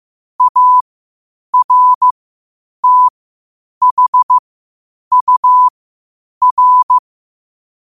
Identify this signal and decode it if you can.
Morse code, heard as a keyed tone, about 15 wpm, spaced as for 8 wpm, 1000 Hz. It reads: ARTHUR
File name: qotdfiles / 8wpm